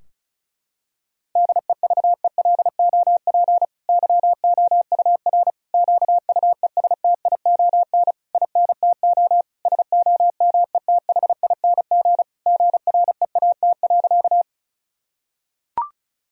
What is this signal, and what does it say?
Morse code, heard as a keyed tone, about 36 wpm, 700 Hz. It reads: DEVELOP YOUR QUESTION INTO SOMETHING GREAT.
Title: Sentences from Top 1000 Words - Review Part 2 - 35wpm